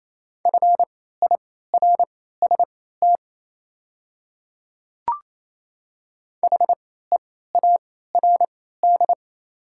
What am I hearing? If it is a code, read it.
Morse code, heard as a keyed tone, about 28 wpm, spaced as for 16 wpm, 700 Hz. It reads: FIRST HEARD
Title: 3x Character Spacing - Top 500 Words - 28wpm